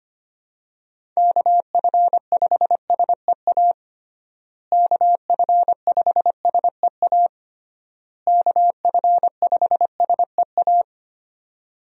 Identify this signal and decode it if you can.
Morse code, heard as a keyed tone, about 25 wpm, 700 Hz. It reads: KF5SEA KF5SEA KF5SEA
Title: USA Call Signs - 6 Characters - Repeated 3x - 25wpm